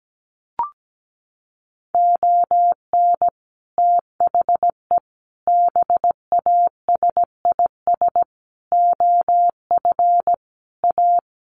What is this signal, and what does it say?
Morse code, heard as a keyed tone, about 17 wpm, 700 Hz. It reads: ON THE BASIS OF A
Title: Sets of 5 Words Encore 17wpm